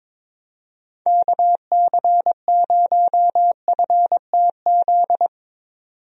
Morse code, heard as a keyed tone, about 22 wpm, 700 Hz. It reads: KC0FTZ
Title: USA Call Signs - 6 Characters - 22wpm